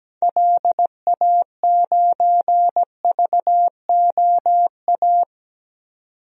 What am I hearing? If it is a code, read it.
Morse code, heard as a keyed tone, about 17 wpm, 700 Hz. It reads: LA9VOA
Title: DXCC-100 Call Signs - 6 Characters - 17wpm